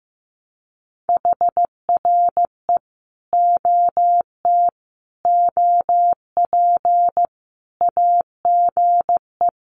Morse code, heard as a keyed tone, about 15 wpm, 700 Hz. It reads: HRE OT OP AGE
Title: Common QSO Phrases 15wpm